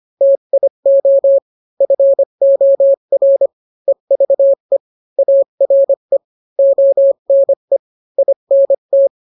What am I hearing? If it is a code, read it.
Morse code, heard as a keyed tone, about 25 wpm, 550 Hz. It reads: TIO FOR EVE ARE ONE INT